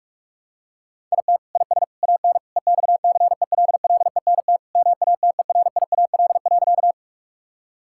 Morse code, heard as a keyed tone, about 42 wpm, 700 Hz. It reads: IT IS AN EXCELLENT MATERIAL.
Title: Sentences from Top 1000 Words - Review Part 5 - 45wpm